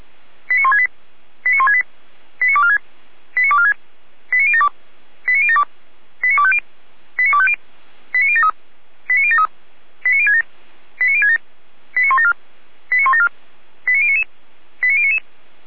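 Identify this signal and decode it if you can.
Five-tone selective call (ZVEI), heard as quick runs of five tones, tones of about 70 ms. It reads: 89168 89168 89246 89246 89072 89072 89250 89250 89073 89073 89067 89067 89174 89174 89099 89099
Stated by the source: ZVEI five-tone selective call received by SDR, address 89240, from a server of Feuerwehr Orlamünde (fire brigade).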